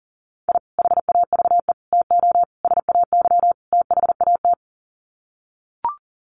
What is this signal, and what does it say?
Morse code, heard as a keyed tone, about 40 wpm, 700 Hz. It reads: I HAVE TO SAY THAT
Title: Sets of 5 Words Encore 40wpm